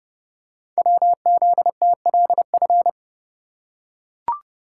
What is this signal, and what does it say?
Morse code, heard as a keyed tone, about 30 wpm, 700 Hz. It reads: WZTLF
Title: Code Groups - 5 Letters - 30wpm